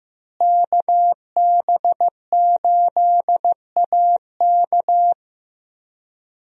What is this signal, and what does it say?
Morse code, heard as a keyed tone, about 15 wpm, 700 Hz. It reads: KB8AK
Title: USA Call Signs - 5 Characters - Repeated 3x - 15wpm